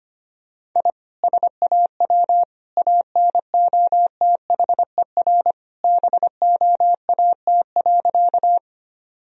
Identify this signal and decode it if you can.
Morse code, heard as a keyed tone, about 25 wpm, 700 Hz. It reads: I SAW ANOTHER BOAT.
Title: Sentences from Top 1000 Words - Review Part 3 - 25wpm